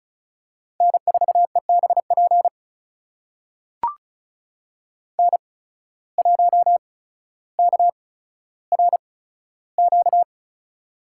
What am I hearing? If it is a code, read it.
Morse code, heard as a keyed tone, about 34 wpm, spaced as for 15 wpm, 700 Hz. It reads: N4EBP N1KRQ
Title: USA Call Signs - 5 Characters - 8x Character Spacing - 35wpm